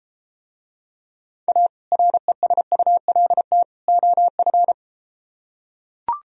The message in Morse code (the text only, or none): A RESULT OF